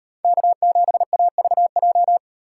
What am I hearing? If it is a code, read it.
Morse code, heard as a keyed tone, about 38 wpm, 700 Hz. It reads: K7AVJ